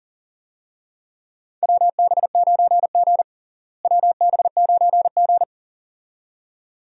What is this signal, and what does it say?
Morse code, heard as a keyed tone, about 37 wpm, 700 Hz. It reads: WB9G WB9G
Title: USA Call Signs - 4 Characters - Repeated 2x - 40wpm